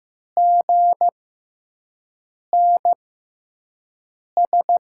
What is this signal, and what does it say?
Morse code, heard as a keyed tone, about 15 wpm, spaced as for 5 wpm, 700 Hz. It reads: GNS
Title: USA Call Signs - 5 Characters - 6x Character Spacing - 15wpm